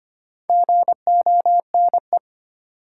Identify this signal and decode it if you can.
Morse code, heard as a keyed tone, about 25 wpm, 700 Hz. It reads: GONE
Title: Sentences from Top 1200 Words - New Word Drill Alt - 25wpm